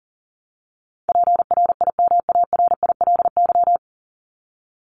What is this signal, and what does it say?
Morse code, heard as a keyed tone, about 40 wpm, 700 Hz. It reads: PRIMARILY
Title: Top 1001-1250 Words 40wpm